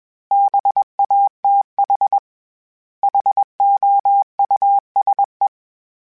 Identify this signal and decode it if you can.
Morse code, heard as a keyed tone, about 21 wpm, 800 Hz. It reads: BATH HOUSE
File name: General / General-mp3s / Houses @20